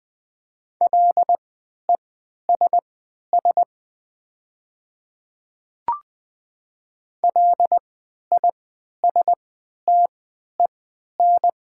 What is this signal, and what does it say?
Morse code, heard as a keyed tone, about 21 wpm, spaced as for 12 wpm, 700 Hz. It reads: LESS LISTEN
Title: 3x Character Spacing - Top 500 Words - 20wpm